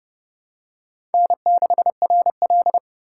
Morse code, heard as a keyed tone, about 29 wpm, 700 Hz. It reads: N6RL